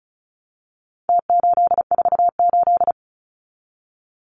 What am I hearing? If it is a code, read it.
Morse code, heard as a keyed tone, about 34 wpm, 700 Hz. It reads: T848
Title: Numbers - Contest Style - 4-Digit 35wpm